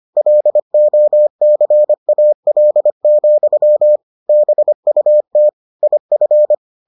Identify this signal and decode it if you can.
Morse code, heard as a keyed tone, about 25 wpm, 600 Hz. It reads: LOCAL, BUT IF